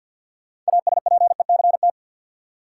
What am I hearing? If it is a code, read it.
Morse code, heard as a keyed tone, about 51 wpm, 700 Hz. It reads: ASPECT
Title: Sentences from Top 1300 Words - New Word Drill - 50wpm